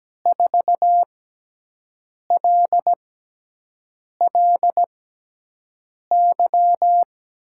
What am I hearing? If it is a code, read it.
Morse code, heard as a keyed tone, about 17 wpm, spaced as for 6 wpm, 700 Hz. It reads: 4LLY
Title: USA Call Signs - 6x Character Spacing - 17wpm